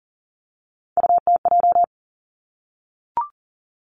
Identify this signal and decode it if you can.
Morse code, heard as a keyed tone, about 40 wpm, 700 Hz. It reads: UTJ
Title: Three Challenging Letter-Number Combinations 40wpm